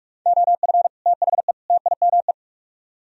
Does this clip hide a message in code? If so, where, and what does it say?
Yes: Morse code from 0.3 to 2.3 s, heard as a keyed tone, about 44 wpm, 700 Hz. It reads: OF THE TIME